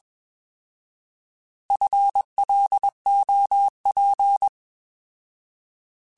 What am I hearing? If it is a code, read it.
Morse code, heard as a keyed tone, about 21 wpm, 800 Hz. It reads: FLOP
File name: Alphabetical / Alphabetical mp3s / Fs @20